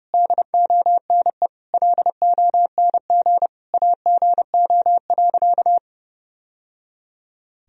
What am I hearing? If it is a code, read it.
Morse code, heard as a keyed tone, about 30 wpm, 700 Hz. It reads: DONE LONG AGO.
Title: Sentences from Top 1000 Words - Review Part 3 - 30wpm